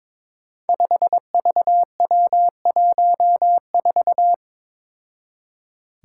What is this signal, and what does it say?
Morse code, heard as a keyed tone, about 22 wpm, 700 Hz. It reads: 5VW14
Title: Five Challenging Letter-Number Combinations 22wpm